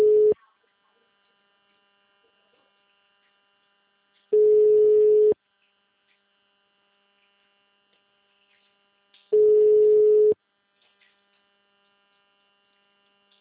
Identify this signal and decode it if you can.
Ringback tone (Europe, 425 Hz).
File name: out- -2002- - - .734